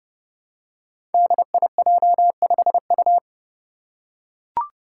Morse code, heard as a keyed tone, about 30 wpm, 700 Hz. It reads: DIJ5U
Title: Five Challenging Letter-Number Combinations 30wpm